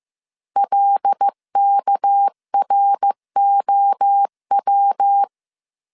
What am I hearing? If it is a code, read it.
Morse code, heard as a keyed tone, about 15 wpm, 800 Hz. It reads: LKROW